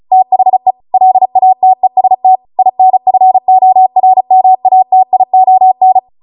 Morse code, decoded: THE LATEST INFORMATION